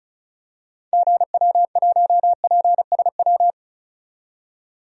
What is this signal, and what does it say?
Morse code, heard as a keyed tone, about 34 wpm, 700 Hz. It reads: GW1PSW
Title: DXCC-100 Call Signs - 6 Characters - 35wpm